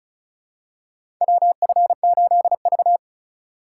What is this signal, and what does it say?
Morse code, heard as a keyed tone, about 34 wpm, 700 Hz. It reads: WF8V
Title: USA Call Signs - 4 Characters - 3x Character Spacing - 35wpm